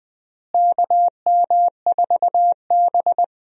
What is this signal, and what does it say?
Morse code, heard as a keyed tone, about 19 wpm, 700 Hz. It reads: KM4B
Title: USA Call Signs - 4 Characters - Repeated 3x - 20wpm